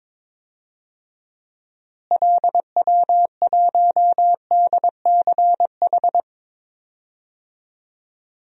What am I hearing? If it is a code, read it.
Morse code, heard as a keyed tone, about 21 wpm, 700 Hz. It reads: LW1DCH
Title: DXCC-100 Call Signs - 6 Characters - Repeated 2x - 22wpm